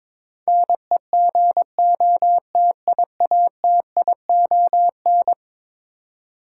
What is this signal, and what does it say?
Morse code, heard as a keyed tone, about 21 wpm, 700 Hz. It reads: NEGOTIATION
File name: practice / 22wpm / Eleven Letter Words - 22wpm